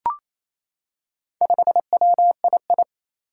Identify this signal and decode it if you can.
Morse code, heard as a keyed tone, about 28 wpm, 700 Hz. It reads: 5WII
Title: Four Challenging Letter-Number Combinations 28wpm